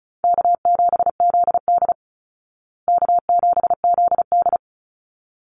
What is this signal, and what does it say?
Morse code, heard as a keyed tone, about 34 wpm, 700 Hz. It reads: K7ZD K7ZD
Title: Easy Call Signs Repeated 2x - 35wpm